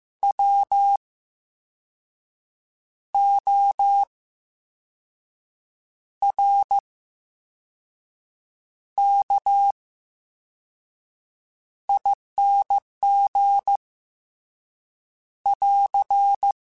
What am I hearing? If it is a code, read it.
Morse code, heard as a keyed tone, about 15 wpm, spaced as for 5 wpm, 800 Hz. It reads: WORKING <AR>
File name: Special INGs / Special INGs-mp3s / Slow ing @13